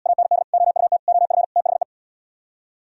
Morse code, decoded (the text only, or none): AND OUT OF THE